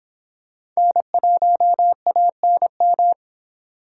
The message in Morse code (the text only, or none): N1ANM